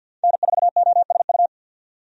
Morse code, heard as a keyed tone, about 49 wpm, 700 Hz. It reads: N4OSU
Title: USA Call Signs - 5 Characters - 50wpm